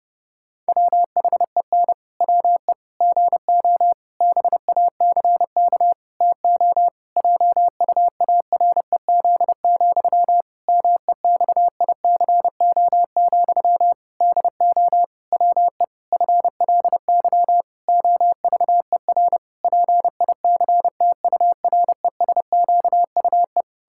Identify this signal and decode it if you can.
Morse code, heard as a keyed tone, about 30 wpm, 700 Hz. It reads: WHEN WE GO BACK TO JUAREZ, MEXICO, DO WE FLY OVER PICTURESQUE
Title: Pangram Challenge v2 - 30wpm